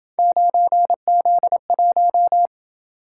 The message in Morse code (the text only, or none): 9Z1